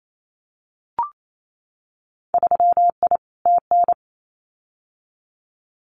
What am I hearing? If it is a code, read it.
Morse code, heard as a keyed tone, about 28 wpm, 700 Hz. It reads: 3I TN
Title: Winter Field Day Exchanges 28wpm